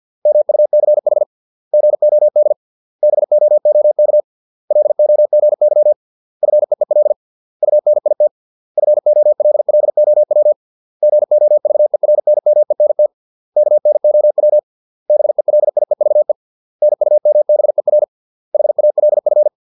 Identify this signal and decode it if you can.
Morse code, heard as a keyed tone, about 50 wpm, 600 Hz. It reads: MUCH GOD BOOK LOGY FEEL UNIT FOLLOW GOVERNMENT KNOW BELIEVE NUMBER HALF